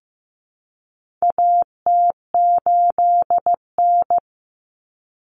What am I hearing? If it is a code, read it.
Morse code, heard as a keyed tone, about 15 wpm, 700 Hz. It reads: AT8N
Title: Common Cut Numbers - 4-Digit 15wpm